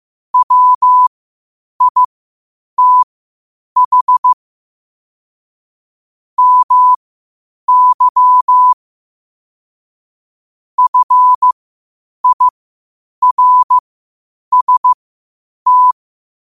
Morse code, heard as a keyed tone, about 15 wpm, spaced as for 8 wpm, 1000 Hz. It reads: WITH MY FIRST
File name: qotdfiles / 8wpm